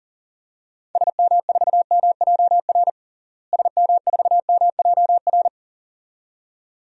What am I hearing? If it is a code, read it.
Morse code, heard as a keyed tone, about 37 wpm, 700 Hz. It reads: SM4MJR SM4MJR
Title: DXCC-100 Call Signs - 6 Characters - Repeated 2x - 40wpm